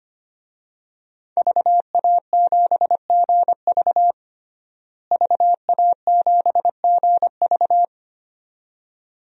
Morse code, heard as a keyed tone, about 25 wpm, 700 Hz. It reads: VA7GV VA7GV
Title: DXCC-100 Call Signs - 5 Characters - Repeated 2x - 25wpm